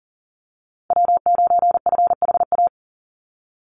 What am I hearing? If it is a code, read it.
Morse code, heard as a keyed tone, about 40 wpm, 700 Hz. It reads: W9FHA
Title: NA SOTA Chaser Callsigns 40wpm